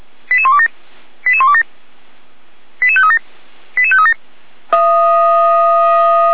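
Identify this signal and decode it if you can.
Five-tone selective call (ZVEI), heard as quick runs of five tones, tones of about 70 ms. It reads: 88127 88127 88537 88537
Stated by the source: ZVEI five-tone call received by SDR.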